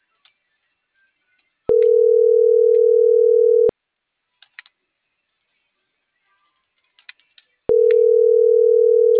Ringback tone (North America), 440 + 480 Hz.